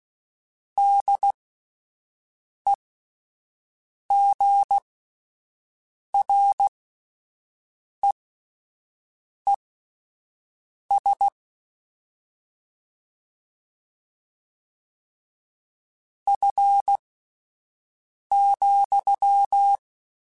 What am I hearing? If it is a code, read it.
Morse code, heard as a keyed tone, about 16 wpm, spaced as for 5 wpm, 800 Hz. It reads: DEGREES F,
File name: Spaced Characters / Spaced Characters-mp3s / Weather 5 22 sp @10